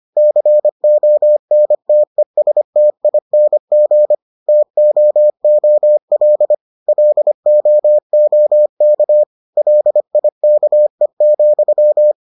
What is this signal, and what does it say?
Morse code, heard as a keyed tone, about 25 wpm, 600 Hz. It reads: CONTESTING TOOL LOOK LIKE,